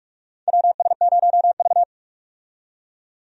Morse code, heard as a keyed tone, about 46 wpm, 700 Hz. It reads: WS0V
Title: USA Call Signs - 4 Characters - 45wpm